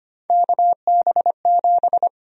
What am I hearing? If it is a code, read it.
Morse code, heard as a keyed tone, about 25 wpm, 700 Hz. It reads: KB7
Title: USA Call Sign Prefixes - 25wpm